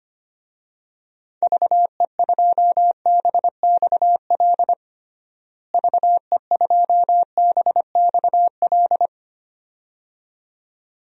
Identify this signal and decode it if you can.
Morse code, heard as a keyed tone, about 25 wpm, 700 Hz. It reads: VE2BXL VE2BXL
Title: DXCC-100 Call Signs - 6 Characters - Repeated 2x - 25wpm